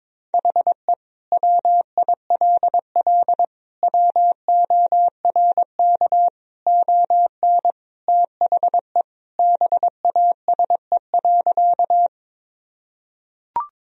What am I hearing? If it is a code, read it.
Morse code, heard as a keyed tone, about 22 wpm, 700 Hz. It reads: HE WILL WORK ON THE BASE.
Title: Sentences from Top 1000 Words - Review Part 1 - 22wpm